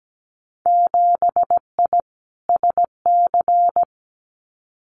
Morse code, heard as a keyed tone, about 17 wpm, 700 Hz. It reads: 7I SC
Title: Winter Field Day Exchanges Encore 17wpm